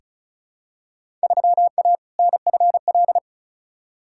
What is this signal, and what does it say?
Morse code, heard as a keyed tone, about 34 wpm, 700 Hz. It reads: 3A NFL
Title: Field Day Exchanges - 35wpm